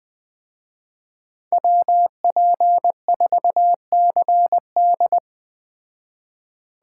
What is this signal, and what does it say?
Morse code, heard as a keyed tone, about 20 wpm, 700 Hz. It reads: WP4CD
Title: Speed Racing - USA 2x2 Call Signs 17wpm